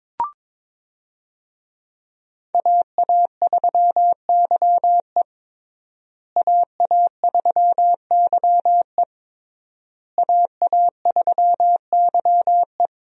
Morse code, heard as a keyed tone, about 21 wpm, 700 Hz. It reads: AA3YE AA3YE AA3YE
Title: USA Call Signs - 5 Characters - Repeated 3x - 22wpm